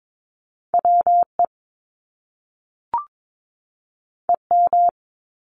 Morse code, heard as a keyed tone, about 22 wpm, 700 Hz. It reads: WE EM
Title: Top Two Letter Combinations in Words 22wpm